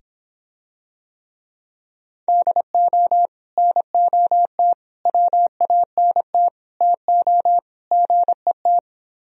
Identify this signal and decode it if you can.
Morse code, heard as a keyed tone, about 26 wpm, 700 Hz. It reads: DO NOT WANT TO GET